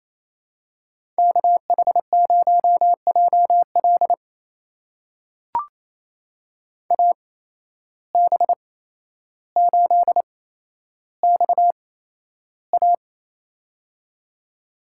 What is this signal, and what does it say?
Morse code, heard as a keyed tone, about 29 wpm, spaced as for 12 wpm, 700 Hz. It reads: KH0JL AB8XA
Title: USA Call Signs - 8x Character Spacing - 28wpm